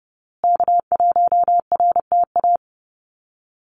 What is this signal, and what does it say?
Morse code, heard as a keyed tone, about 30 wpm, 700 Hz. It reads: K1RTA
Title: USA Call Signs Repeated 2x 30wpm